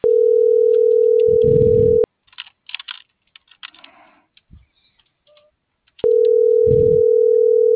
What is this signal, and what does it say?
Ringback tone (North America), 440 + 480 Hz.